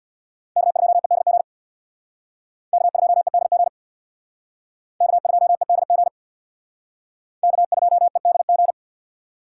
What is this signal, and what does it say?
Morse code, heard as a keyed tone, about 64 wpm, 700 Hz. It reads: K2EDG K2EDG K2EDG K2EDG